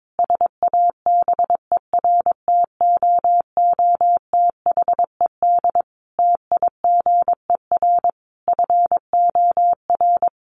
Morse code, heard as a keyed tone, about 22 wpm, 700 Hz. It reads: SABERTOOTHED TIGER FOR
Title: A Sound Of Thunder 22wpm